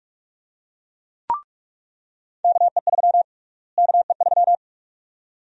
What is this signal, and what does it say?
Morse code, heard as a keyed tone, about 47 wpm, 700 Hz. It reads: KE3 KE3
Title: USA Call Sign Prefixes - Repeated 2x - 45wpm